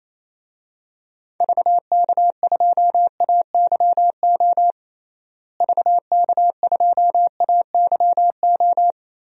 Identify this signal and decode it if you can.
Morse code, heard as a keyed tone, about 28 wpm, 700 Hz. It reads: VK2AYO VK2AYO
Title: DXCC-100 Call Signs - 6 Characters - Repeated 3x - 28wpm